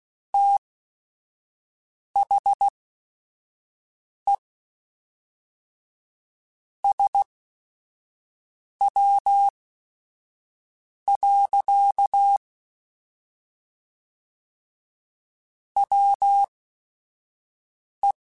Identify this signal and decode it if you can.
Morse code, heard as a keyed tone, about 16 wpm, spaced as for 5 wpm, 800 Hz. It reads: THE SW. WE
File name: Weather Sets / Weather Sets-mp3s / Weather 5 5 @5